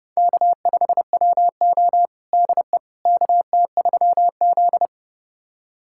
Morse code, heard as a keyed tone, about 30 wpm, 700 Hz. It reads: K5WO DE KT3Z
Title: USA Callsign DE Callsign - 4 Characters - 30wpm